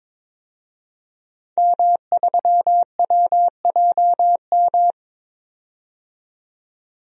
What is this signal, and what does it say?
Morse code, heard as a keyed tone, about 22 wpm, 700 Hz. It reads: M3WJM
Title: DXCC-100 Call Signs - 5 Characters - 22wpm